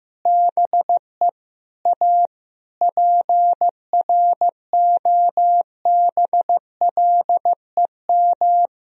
Morse code, heard as a keyed tone, about 15 wpm, 700 Hz. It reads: BE A PROBLEM
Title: Sentences from Top 1000 Words - Review Part 2 - 15wpm